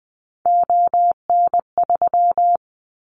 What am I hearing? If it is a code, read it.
Morse code, heard as a keyed tone, about 20 wpm, 700 Hz. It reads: ON3
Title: Callsign Prefixes for Top 100 DXCC - 20wpm